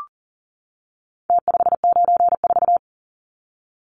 Morse code, heard as a keyed tone, about 40 wpm, 700 Hz. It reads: T594
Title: Numbers - Contest Style - 4-Digit 40wpm